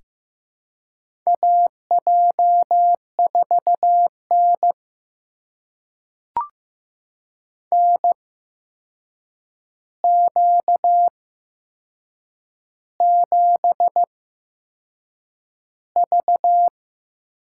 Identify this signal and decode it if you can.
Morse code, heard as a keyed tone, about 15 wpm, spaced as for 6 wpm, 700 Hz. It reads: AJ4N NQ7V
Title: USA Call Signs - 4 Characters - 8x Character Spacing - 15wpm